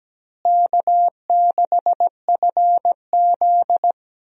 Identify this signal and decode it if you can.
Morse code, heard as a keyed tone, about 17 wpm, 700 Hz. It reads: K6FZ
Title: USA Call Signs - 4 Characters - Repeated 2x - 17wpm